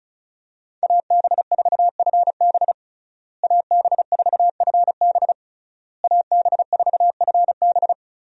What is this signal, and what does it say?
Morse code, heard as a keyed tone, about 36 wpm, 700 Hz. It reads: AB4FB AB4FB AB4FB
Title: USA Call Signs - 5 Characters - Repeated 3x - 35wpm